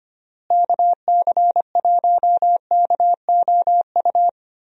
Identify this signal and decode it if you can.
Morse code, heard as a keyed tone, about 25 wpm, 700 Hz. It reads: KC1KOU
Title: USA Call Signs - 6 Characters - Repeated 3x - 25wpm